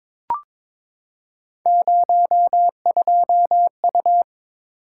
Morse code, heard as a keyed tone, about 22 wpm, 700 Hz. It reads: 02U
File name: practice / 22wpm / Three Letter-Number Combinations - 22wpm